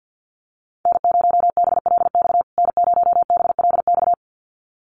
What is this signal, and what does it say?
Morse code, heard as a keyed tone, about 50 wpm, 700 Hz. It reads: N0BLX N0BLX
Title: USA Call Signs Repeated 2x 50wpm